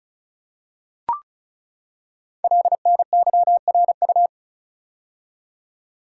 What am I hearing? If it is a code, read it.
Morse code, heard as a keyed tone, about 36 wpm, 700 Hz. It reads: LNYRU
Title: Code Groups - 5 Letter:Numbers - 35wpm